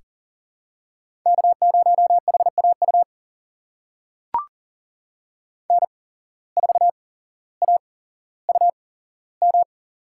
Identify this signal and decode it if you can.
Morse code, heard as a keyed tone, about 40 wpm, spaced as for 17 wpm, 700 Hz. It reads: K0HAU N4AUM
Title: USA Call Signs - 5 Characters - 8x Character Spacing - 40wpm